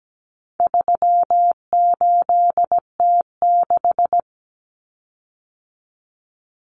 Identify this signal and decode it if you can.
Morse code, heard as a keyed tone, about 17 wpm, 700 Hz. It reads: 38T6
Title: Common Cut Numbers - 4-Digit 17wpm